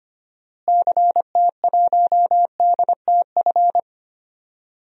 Morse code, heard as a keyed tone, about 25 wpm, 700 Hz. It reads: CT1DTF
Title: DXCC-100 Call Signs - 6 Characters - Repeated 2x - 25wpm